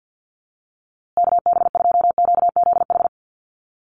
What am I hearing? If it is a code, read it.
Morse code, heard as a keyed tone, about 50 wpm, 700 Hz. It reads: KB2QZH
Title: USA Call Signs Repeated 2x 50wpm